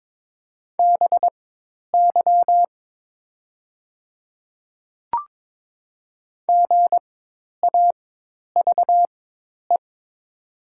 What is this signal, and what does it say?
Morse code, heard as a keyed tone, about 22 wpm, spaced as for 10 wpm, 690 Hz. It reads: BY GAVE
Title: 4x Character Spacing - Top 500 Words - 22wpm